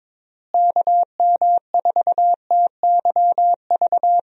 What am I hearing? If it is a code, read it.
Morse code, heard as a keyed tone, about 21 wpm, 700 Hz. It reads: KM4TYV
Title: USA Call Signs - 6 Characters - Repeated 3x - 22wpm